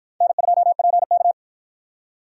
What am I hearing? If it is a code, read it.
Morse code, heard as a keyed tone, about 48 wpm, 700 Hz. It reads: N2PK